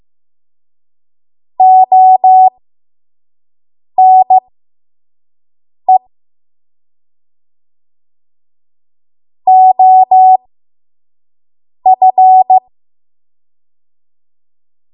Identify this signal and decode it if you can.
Morse code, heard as a keyed tone, about 15 wpm, spaced as for 5 wpm, 750 Hz. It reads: ONE OF
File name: Morse / 05WPM